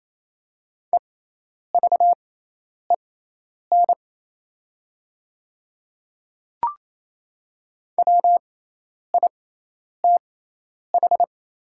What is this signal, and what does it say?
Morse code, heard as a keyed tone, about 28 wpm, spaced as for 10 wpm, 700 Hz. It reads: EVEN WITH
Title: 6x Character Spacing - Top 200 Words - 28wpm